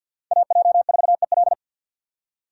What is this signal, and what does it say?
Morse code, heard as a keyed tone, about 47 wpm, 700 Hz. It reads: AJ4EL
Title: USA Call Signs - 5 Characters - 3x Character Spacing - 50wpm